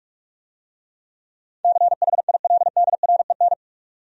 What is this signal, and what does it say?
Morse code, heard as a keyed tone, about 46 wpm, 700 Hz. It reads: CHILDREN